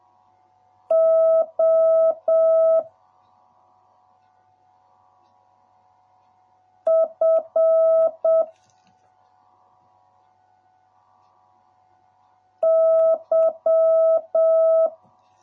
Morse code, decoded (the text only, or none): O F Y